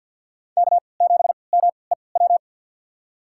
Morse code, heard as a keyed tone, about 53 wpm, spaced as for 29 wpm, 690 Hz. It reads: K7MEW
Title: USA Call Signs - 5 Characters - 3x Character Spacing - 50wpm